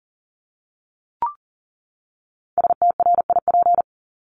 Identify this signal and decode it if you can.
Morse code, heard as a keyed tone, about 40 wpm, 700 Hz. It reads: STRIP
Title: Top 1501-1750 Words 40wpm